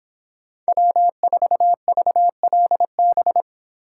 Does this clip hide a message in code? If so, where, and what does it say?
Yes: Morse code from 0.7 to 3.4 s, heard as a keyed tone, about 25 wpm, 700 Hz. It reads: W4VLB